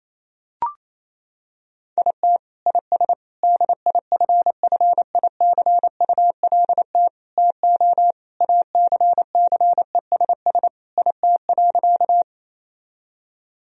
Morse code, heard as a keyed tone, about 27 wpm, 700 Hz. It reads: IT IS DIFFICULT TO ACCESS IT.
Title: Sentences from Top 1000 Words - Review Part 4 - 28wpm